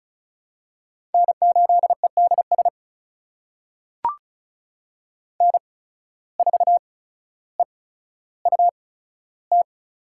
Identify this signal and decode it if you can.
Morse code, heard as a keyed tone, about 36 wpm, spaced as for 15 wpm, 700 Hz. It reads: N8EDS N4EUT
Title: USA Call Signs - 5 Characters - 8x Character Spacing - 35wpm